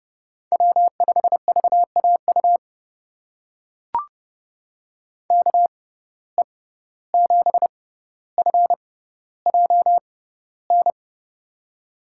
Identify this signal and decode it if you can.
Morse code, heard as a keyed tone, about 32 wpm, spaced as for 15 wpm, 700 Hz. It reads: W5VAU KE7FJN
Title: USA Call Signs - 6x Character Spacing - 30wpm